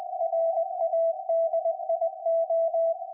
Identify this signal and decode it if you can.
Morse code, heard as a keyed tone, about 20 wpm, 660 Hz. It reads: RADIO
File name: IC-7100